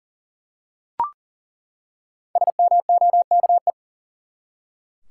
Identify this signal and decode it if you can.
Morse code, heard as a keyed tone, about 40 wpm, 700 Hz. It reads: SMOKE
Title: Five Letter Words - 40wpm